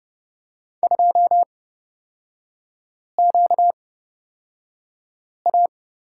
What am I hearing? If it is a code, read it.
Morse code, heard as a keyed tone, about 30 wpm, spaced as for 5 wpm, 700 Hz. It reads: 2QA
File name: Lesson 38e - Letter q - Three Characters 30wpm